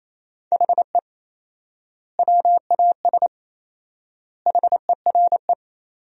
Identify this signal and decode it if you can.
Morse code, heard as a keyed tone, about 28 wpm, 700 Hz. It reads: HE WAS HERE
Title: 4x Word-Spacing - Sentences from Top 500 Words - 28wpm